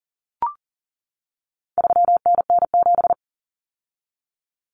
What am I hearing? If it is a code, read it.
Morse code, heard as a keyed tone, about 40 wpm, 700 Hz. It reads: 3NN7
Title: Common Cut Numbers - 4-Digit 40wpm